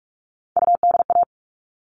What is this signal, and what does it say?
Morse code, heard as a keyed tone, about 44 wpm, 700 Hz. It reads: UDA
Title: All Cut Numbers - 3-Digit 45wpm